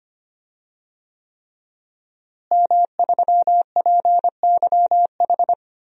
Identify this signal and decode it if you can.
Morse code, heard as a keyed tone, about 25 wpm, 700 Hz. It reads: M3PYH
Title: DXCC-100 Call Signs - 5 Characters - Repeated 3x - 25wpm